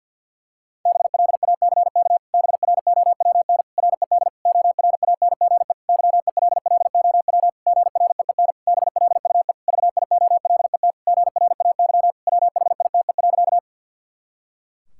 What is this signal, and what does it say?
Morse code, heard as a keyed tone, about 50 wpm, 700 Hz. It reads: BLACK BROWN RED ORANGE YELLOW GREEN BLUE VIOLET GRAY WHITE.
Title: Pangram Challenge v2 - 50wpm